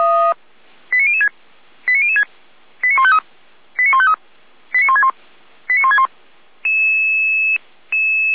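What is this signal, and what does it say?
Five-tone selective call (ZVEI), heard as quick runs of five tones, tones of about 70 ms. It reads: 89006 89006 89153 89153 89161 89161
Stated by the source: ZVEI five-tone selective call 89078 received by SDR.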